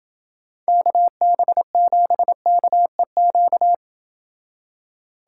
Morse code, heard as a keyed tone, about 28 wpm, 700 Hz. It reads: KB7KEQ